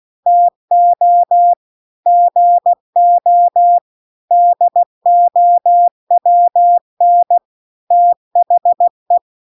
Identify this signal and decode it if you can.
Morse code, heard as a keyed tone, about 16 wpm, 700 Hz. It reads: TO GO DOWN THE